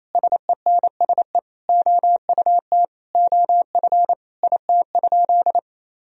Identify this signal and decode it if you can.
Morse code, heard as a keyed tone, about 28 wpm, 700 Hz. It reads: SENSE OUT OF IT?
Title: Sentences from Top 1000 Words - Review Part 4 - 28wpm